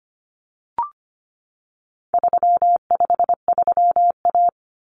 Morse code, heard as a keyed tone, about 25 wpm, 700 Hz. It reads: 353A
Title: Common Cut Numbers - 4-Digit 25wpm